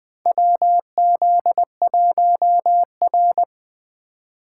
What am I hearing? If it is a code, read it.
Morse code, heard as a keyed tone, about 21 wpm, 700 Hz. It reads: WZ1R
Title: USA Call Signs - 3x Character Spacing - 20wpm